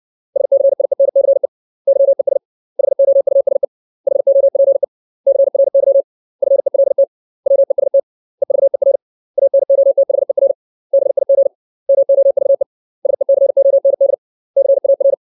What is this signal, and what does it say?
Morse code, heard as a keyed tone, about 60 wpm, 550 Hz. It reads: SCIENCE YES HOUSE HOPE CRY FELT WEST EVER ANOTHER BIG MOVE SECOND CAR
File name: 632 words / 632 Words 60WPM